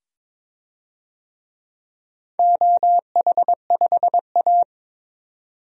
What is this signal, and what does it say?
Morse code, heard as a keyed tone, about 22 wpm, 700 Hz. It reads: OH5A
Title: DXCC-100 Call Signs - 4 Characters - 22wpm